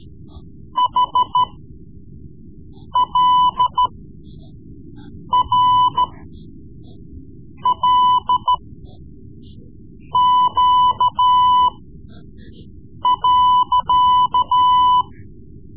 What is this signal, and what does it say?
Morse code, heard as a keyed tone, about 13 wpm, spaced as for 5 wpm, 1000 Hz. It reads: HLRLQ.
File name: Morse Lessons / Morse Lessons / Tape-3B